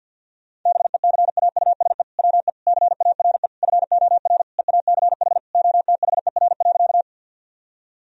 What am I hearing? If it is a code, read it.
Morse code, heard as a keyed tone, about 50 wpm, 700 Hz. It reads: BECAUSE WE CARE FOR EACH OTHER.
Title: Sentences from Top 1000 Words - Review Part 3 - 50wpm